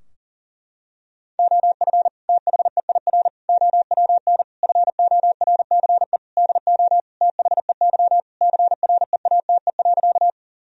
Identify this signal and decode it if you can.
Morse code, heard as a keyed tone, about 40 wpm, 700 Hz. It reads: OF THEIR OWN FORCE DO THEY CREATE.
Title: Sentences from Top 1000 Words - Review Part 3 - 40wpm